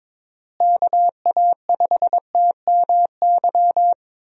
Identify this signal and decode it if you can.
Morse code, heard as a keyed tone, about 22 wpm, 700 Hz. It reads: KA5TMY
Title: USA Call Signs - 6 Characters - Repeated 3x - 22wpm